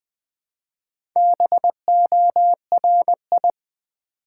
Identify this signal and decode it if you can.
Morse code, heard as a keyed tone, about 20 wpm, 690 Hz. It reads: BORI
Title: Build A Word - 2 Letters At A Time - 6-Letter Words - 20wpm